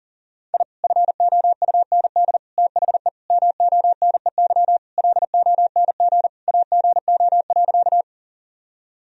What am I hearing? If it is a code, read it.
Morse code, heard as a keyed tone, about 40 wpm, 700 Hz. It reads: I FOUND THE MONEY LONG AGO.
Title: Sentences from Top 1000 Words - Review Part 3 - 40wpm